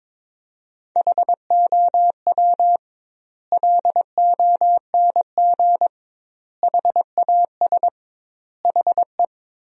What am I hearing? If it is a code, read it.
Morse code, heard as a keyed tone, about 21 wpm, 700 Hz. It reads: HOW LONG HAS HE
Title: 2x Word-Spacing - Sentences from Top 100 Words - 22wpm